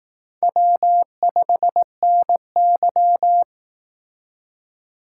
Morse code, heard as a keyed tone, about 18 wpm, 700 Hz. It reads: W5NY
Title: Speed Racing - USA Call Signs 15wpm